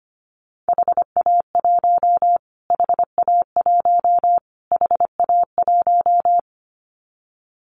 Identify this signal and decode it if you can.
Morse code, heard as a keyed tone, about 25 wpm, 700 Hz. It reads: HA1 HA1 HA1
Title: Callsign Prefixes for Top 25 DXCC - Repeated 3x - 25wpm